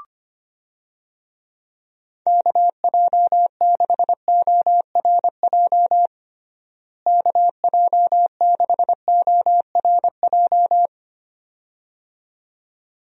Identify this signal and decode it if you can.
Morse code, heard as a keyed tone, about 25 wpm, 700 Hz. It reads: KJ6ORJ KJ6ORJ
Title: USA Call Signs - 6 Characters - Repeated 2x - 25wpm